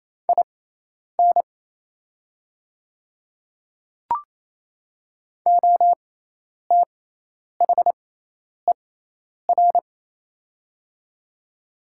Morse code, heard as a keyed tone, about 28 wpm, spaced as for 10 wpm, 700 Hz. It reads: IN OTHER
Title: 6x Character Spacing - Top 100 Words - 28wpm